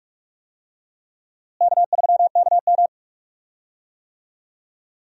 Morse code, heard as a keyed tone, about 48 wpm, 700 Hz. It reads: K3KM
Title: USA Call Signs - 4 Characters - Repeated 3x - 45wpm